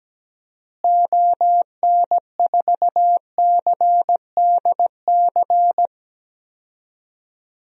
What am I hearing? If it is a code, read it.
Morse code, heard as a keyed tone, about 17 wpm, 700 Hz. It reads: ON4CDC
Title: DXCC-100 Call Signs - 6 Characters - Repeated 3x - 17wpm